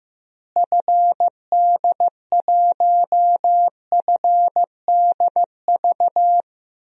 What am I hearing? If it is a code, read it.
Morse code, heard as a keyed tone, about 16 wpm, 700 Hz. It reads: FD1FDV
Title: DXCC-100 Call Signs - 6 Characters - Repeated 3x - 15wpm